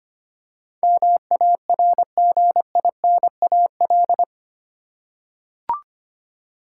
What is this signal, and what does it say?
Morse code, heard as a keyed tone, about 25 wpm, 700 Hz. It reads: MARGINAL
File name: 8 Letter Words - First 3 Letters Given - 25wpm